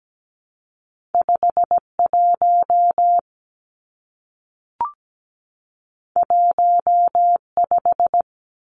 Morse code, heard as a keyed tone, about 17 wpm, 700 Hz. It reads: 51 15
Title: Numbers - 2-Digit 17wpm